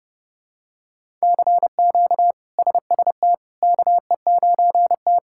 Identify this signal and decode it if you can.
Morse code, heard as a keyed tone, about 31 wpm, 700 Hz. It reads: CQ SST KE9T
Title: K1USN SST QSO Practice - 30wpm